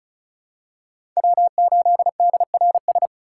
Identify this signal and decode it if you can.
Morse code, heard as a keyed tone, about 36 wpm, 700 Hz. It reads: W8DRS
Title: USA Call Signs - 4x Character Spacing - 35wpm